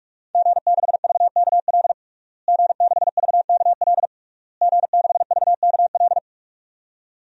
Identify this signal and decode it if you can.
Morse code, heard as a keyed tone, about 47 wpm, 700 Hz. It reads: G6VKL G6VKL G6VKL
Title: DXCC-100 Call Signs - 5 Characters - Repeated 3x - 45wpm